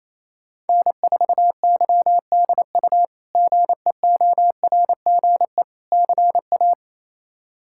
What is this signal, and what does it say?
Morse code, heard as a keyed tone, about 27 wpm, 700 Hz. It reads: N4YDU GEORGE CA
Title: K1USN SST QSO Practice - 28wpm